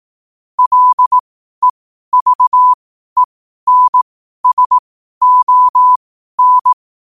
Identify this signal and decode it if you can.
Morse code, heard as a keyed tone, about 18 wpm, spaced as for 12 wpm, 1000 Hz. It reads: LEVENSON
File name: qotdfiles / 12wpm